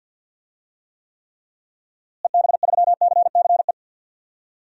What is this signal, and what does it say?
Morse code, heard as a keyed tone, about 49 wpm, 700 Hz. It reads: EB3CCE
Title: DXCC-100 Call Signs - 6 Characters - Repeated 3x - 50wpm